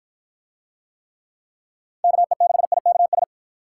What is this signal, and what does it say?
Morse code, heard as a keyed tone, about 53 wpm, 700 Hz. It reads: KE6IKS